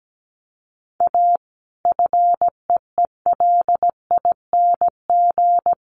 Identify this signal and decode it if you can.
Morse code, heard as a keyed tone, about 17 wpm, 700 Hz. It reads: A FEELING
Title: Sentences from Top 1200 Words 17wpm